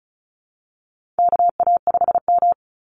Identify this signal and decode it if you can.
Morse code, heard as a keyed tone, about 34 wpm, 700 Hz. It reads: KA5M
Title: Call Signs - 35wpm